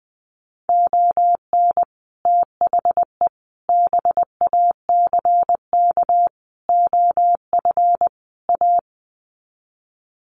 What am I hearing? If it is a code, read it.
Morse code, heard as a keyed tone, about 20 wpm, 700 Hz. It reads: ON THE BACK OF A
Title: Sets of 5 Words Encore2 20wpm